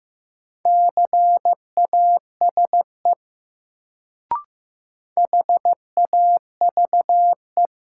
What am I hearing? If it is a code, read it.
Morse code, heard as a keyed tone, about 15 wpm, 700 Hz. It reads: CASE HAVE
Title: Sets of 2 Words - New Word Drill - 15wpm